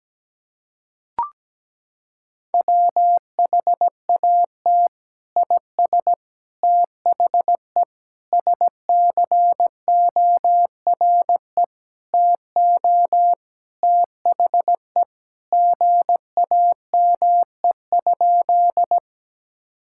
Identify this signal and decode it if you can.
Morse code, heard as a keyed tone, about 17 wpm, 700 Hz. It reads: WHAT IS THE SCORE TO THE GAME?
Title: Sentences from Top 1600 Words 17wpm